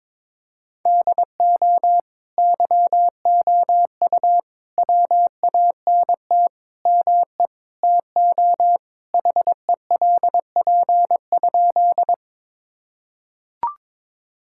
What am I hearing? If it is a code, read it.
Morse code, heard as a keyed tone, about 22 wpm, 700 Hz. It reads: DO YOU WANT ME TO HELP?
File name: practice / 22wpm / Sentences with 4-Word Phrases - 22wpm